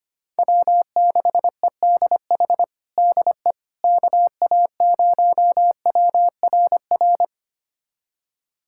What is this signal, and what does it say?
Morse code, heard as a keyed tone, about 25 wpm, 700 Hz. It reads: W6EDH DE KA0WRR
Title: USA Callsign DE Callsign 25wpm